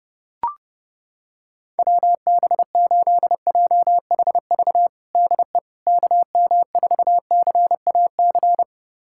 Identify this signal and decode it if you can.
Morse code, heard as a keyed tone, about 32 wpm, 700 Hz. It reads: WB8JHV DE KM4CAC
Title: USA Callsign DE Callsign 30wpm